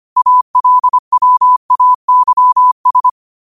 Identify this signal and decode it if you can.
Morse code, heard as a keyed tone, about 25 wpm, 1000 Hz. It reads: ALWAYS